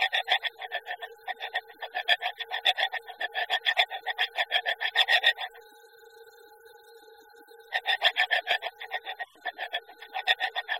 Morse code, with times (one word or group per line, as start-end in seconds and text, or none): none